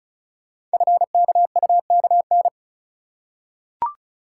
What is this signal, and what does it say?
Morse code, heard as a keyed tone, about 36 wpm, 700 Hz. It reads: FKUKN
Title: Code Groups - 5 Letters - 35wpm